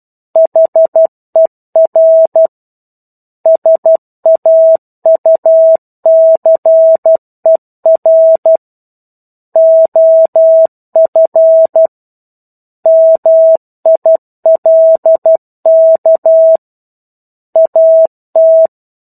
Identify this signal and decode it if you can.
Morse code, heard as a keyed tone, about 12 wpm, 650 Hz. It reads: HER SAUCER OF MILK AT
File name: 01-Chapter-01 12 WPM